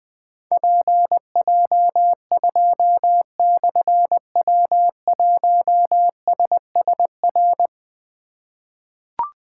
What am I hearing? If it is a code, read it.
Morse code, heard as a keyed tone, about 21 wpm, 700 Hz. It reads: PJ2/W1SSR
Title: Difficult Call Signs 20wpm